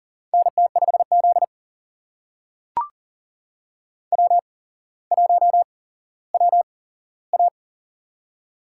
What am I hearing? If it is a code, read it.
Morse code, heard as a keyed tone, about 38 wpm, spaced as for 17 wpm, 700 Hz. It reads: NT5Z W1WA
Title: USA Call Signs - 4 Characters - 8x Character Spacing - 40wpm